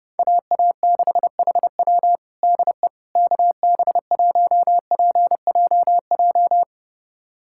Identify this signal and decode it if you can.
Morse code, heard as a keyed tone, about 30 wpm, 700 Hz. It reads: AA6HW DE KB1PJJ
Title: USA Callsign DE Callsign 30wpm